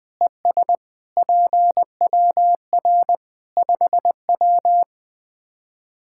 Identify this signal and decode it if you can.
Morse code, heard as a keyed tone, about 21 wpm, 700 Hz. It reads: ES PWR 5W
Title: Practice QSO - Exchange 2 - Info - 20wpm